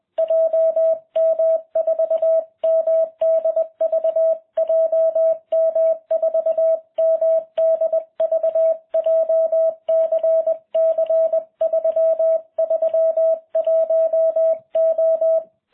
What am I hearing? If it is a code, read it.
Morse code, heard as a keyed tone, about 20 wpm, 660 Hz. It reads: JM4MDV JM4MDV JCC 331O